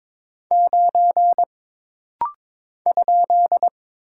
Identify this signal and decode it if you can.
Morse code, heard as a keyed tone, about 22 wpm, 700 Hz. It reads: 9 ?
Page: This rapid-fire timing is geared to push your ICR (Instant Character Recognition) skills.